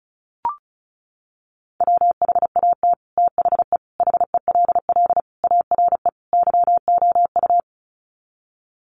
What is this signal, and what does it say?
Morse code, heard as a keyed tone, about 34 wpm, 700 Hz. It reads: WHAT THE HELL ARE YOU
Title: Sets of 5 Words Encore 35wpm